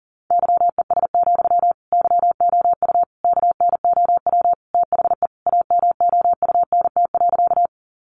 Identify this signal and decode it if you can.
Morse code, heard as a keyed tone, about 37 wpm, 700 Hz. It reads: YES, YOU KNOW THE AMOUNT.
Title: Sentences from Top 700 Words 40wpm